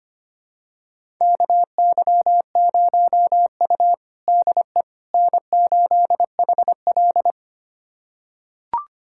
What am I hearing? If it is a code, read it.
Morse code, heard as a keyed tone, about 25 wpm, 700 Hz. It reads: KY0U DE N8HL
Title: USA Callsign DE Callsign - 4 Characters - 25wpm